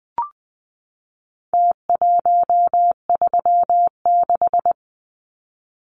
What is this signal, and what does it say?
Morse code, heard as a keyed tone, about 20 wpm, 700 Hz. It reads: T136
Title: Numbers - Contest Style - 4-Digit 20wpm